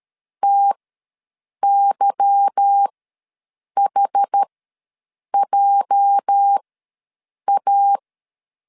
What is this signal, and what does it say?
Morse code, heard as a keyed tone, about 13 wpm, spaced as for 7 wpm, 800 Hz. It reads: TYHJA
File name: LEZIONI / 8 YJ45 / 8 YJ ws 12wpm 3342